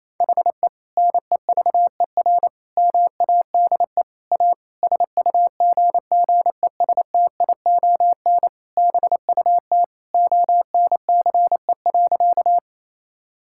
Morse code, heard as a keyed tone, about 28 wpm, 700 Hz. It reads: HE NEVER MADE A SUGGESTION BUT ONCE.
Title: Sentences from Top 1600 Words 28wpm